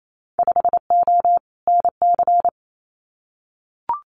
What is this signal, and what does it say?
Morse code, heard as a keyed tone, about 28 wpm, 700 Hz. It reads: 5O NC
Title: Winter Field Day Exchanges 28wpm